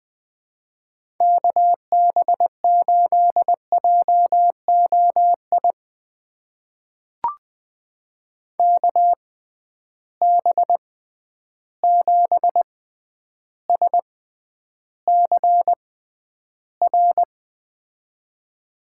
Morse code, heard as a keyed tone, about 20 wpm, spaced as for 10 wpm, 700 Hz. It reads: KB8JOI KB7SCR
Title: USA Call Signs - 6x Character Spacing - 20wpm